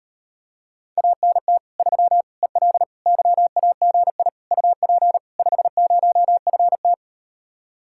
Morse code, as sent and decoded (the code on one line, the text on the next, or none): .- -. - / ...-- / . .-.. / -.-- .- --. .. / ..- .--. / ..... ----- ..-. -
ANT 3 EL YAGI UP 50FT